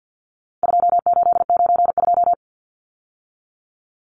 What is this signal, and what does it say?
Morse code, heard as a keyed tone, about 50 wpm, 700 Hz. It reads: 2892
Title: Numbers - 4-Digit 50wpm